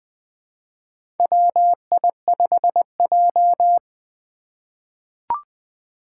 Morse code, heard as a keyed tone, about 20 wpm, 700 Hz. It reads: WI5J
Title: Four Challenging Letter-Number Combinations 20wpm